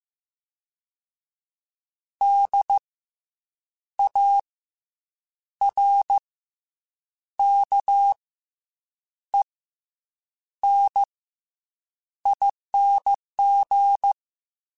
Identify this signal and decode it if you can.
Morse code, heard as a keyed tone, about 15 wpm, 800 Hz. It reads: D A R K E N ING
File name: Special INGs / Special INGs-mp3s / SlowToFaster ing @13